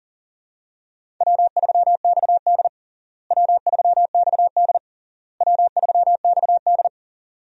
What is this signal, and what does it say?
Morse code, heard as a keyed tone, about 37 wpm, 700 Hz. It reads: W3XD W3XD W3XD
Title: USA Call Signs - 4 Characters - Repeated 3x - 40wpm